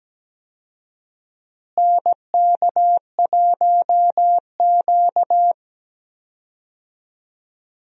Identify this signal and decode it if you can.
Morse code, heard as a keyed tone, about 17 wpm, 700 Hz. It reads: NK1Q
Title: USA Call Signs - 4 Characters - Repeated 3x - 17wpm